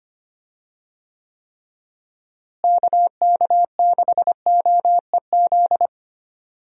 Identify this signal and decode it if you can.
Morse code, heard as a keyed tone, about 25 wpm, 700 Hz. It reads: KK6OEZ
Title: USA Call Signs - 6 Characters - Repeated 3x - 25wpm